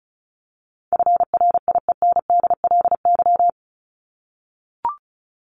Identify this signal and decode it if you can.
Morse code, heard as a keyed tone, about 34 wpm, 700 Hz. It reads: FRIENDLY
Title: Top 1751-2000 Words 35wpm